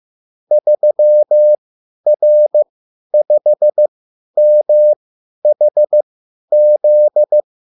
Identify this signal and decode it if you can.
Morse code, heard as a keyed tone, about 15 wpm, spaced as for 10 wpm, 600 Hz. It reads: 3R5MHZ
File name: QSL / CW lesson